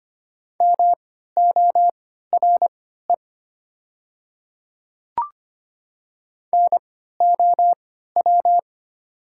Morse code, heard as a keyed tone, about 25 wpm, spaced as for 14 wpm, 700 Hz. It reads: MORE NOW
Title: 3x Character Spacing - Top 500 Words - 25wpm